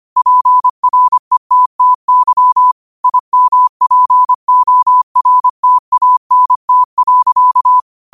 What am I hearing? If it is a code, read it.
Morse code, heard as a keyed tone, about 25 wpm, 1000 Hz. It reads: PRETTY IMPORTANT.
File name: qotdfiles / 25wpm